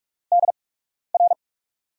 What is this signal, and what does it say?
Morse code, heard as a keyed tone, about 44 wpm, spaced as for 12 wpm, 700 Hz. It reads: DR
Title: USA Call Signs - 8x Character Spacing - 45wpm